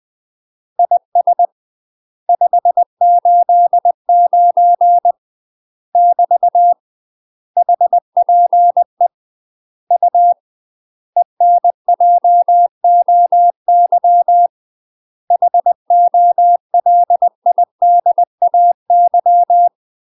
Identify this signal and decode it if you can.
Morse code, heard as a keyed tone, about 20 wpm, 700 Hz. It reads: IS 589 <BT> HPE U ENJOY HOLIDAY
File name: UK MASTER QSOs TRAINING/20WPM